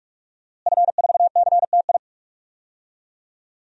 Morse code, heard as a keyed tone, about 46 wpm, 700 Hz. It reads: F4CTI
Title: DXCC-100 Call Signs - 5 Characters - Repeated 3x - 45wpm